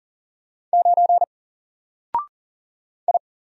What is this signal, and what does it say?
Morse code, heard as a keyed tone, about 40 wpm, 700 Hz. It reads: 9 I
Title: Single Letter-Number Alt 40wpm